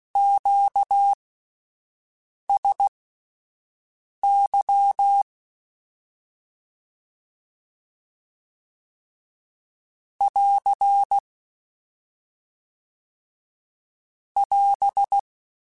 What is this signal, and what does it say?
Morse code, heard as a keyed tone, about 16 wpm, spaced as for 5 wpm, 800 Hz. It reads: QSY <AR> <AS>